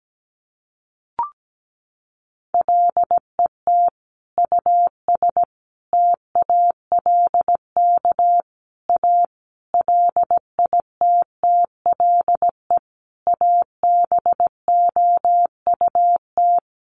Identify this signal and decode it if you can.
Morse code, heard as a keyed tone, about 17 wpm, 700 Hz. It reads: LET US TALK A LITTLE ABOUT
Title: Sentences from Top 400 Words 17wpm